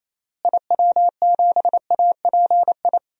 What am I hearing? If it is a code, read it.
Morse code, heard as a keyed tone, about 28 wpm, 700 Hz. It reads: IW7API
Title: DXCC-100 Call Signs - 6 Characters - Repeated 3x - 28wpm